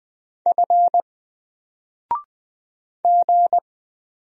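Morse code, heard as a keyed tone, about 20 wpm, 700 Hz. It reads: F G
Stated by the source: Single letter and number Morse drill at 20 wpm, from N4LCD.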